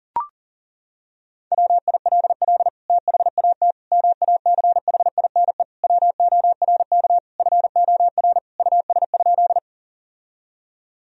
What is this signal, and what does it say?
Morse code, heard as a keyed tone, about 40 wpm, 700 Hz. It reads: WILL THAT MACHINE WORK FOR US?
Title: Sentences from Top 1000 Words - Review Part 2 - 40wpm